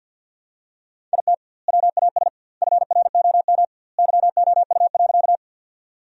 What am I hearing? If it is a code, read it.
Morse code, heard as a keyed tone, about 50 wpm, 700 Hz. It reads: IT WAS FROM YOU.
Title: 2x Word-Spacing - Sentences from Top 100 Words - 50wpm